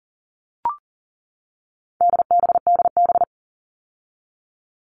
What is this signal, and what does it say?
Morse code, heard as a keyed tone, about 40 wpm, 700 Hz. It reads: DBDB
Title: All Cut Numbers - 4-Digit 40wpm